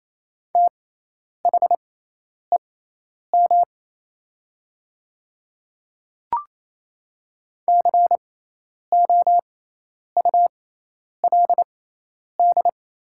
Morse code, heard as a keyed tone, about 28 wpm, spaced as for 10 wpm, 700 Hz. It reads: THEM COULD
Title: 6x Character Spacing - Top 100 Words - 28wpm